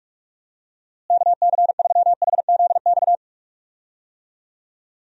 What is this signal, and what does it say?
Morse code, heard as a keyed tone, about 46 wpm, 700 Hz. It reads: KC3HZX
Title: USA Call Signs - 6 Characters - Repeated 3x - 45wpm